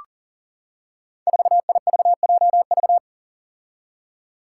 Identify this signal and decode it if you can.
Morse code, heard as a keyed tone, about 40 wpm, 700 Hz. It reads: 4IVJV
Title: Five Challenging Letter-Number Combinations 40wpm